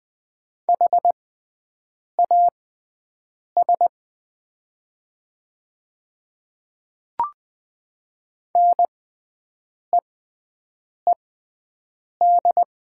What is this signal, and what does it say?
Morse code, heard as a keyed tone, about 20 wpm, spaced as for 7 wpm, 700 Hz. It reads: HAS NEED
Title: 6x Character Spacing - Top 200 Words - 20wpm